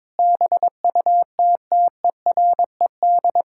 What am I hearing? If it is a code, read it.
Morse code, heard as a keyed tone, about 21 wpm, 700 Hz. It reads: BUTTERED
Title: Pangram Challenge v2 - 22wpm